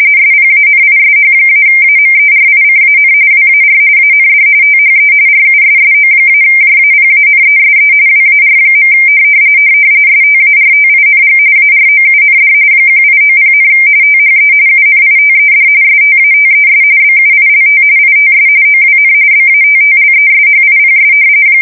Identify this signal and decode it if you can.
RTTY, about 45.45 baud. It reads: SPANISH-AMERICAN WAR AND THE RUSSO-JAPANESE WAR. STARTING AROUND 1890, THE UNITED STATES BUILT ITS OWN BATTLESHIPS
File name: fsk170 / battleshipnj